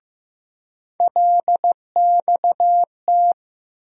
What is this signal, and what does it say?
Morse code, heard as a keyed tone, about 15 wpm, 700 Hz. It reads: LXT
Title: USA Call Sign Postfixes - 3 Letters 15wpm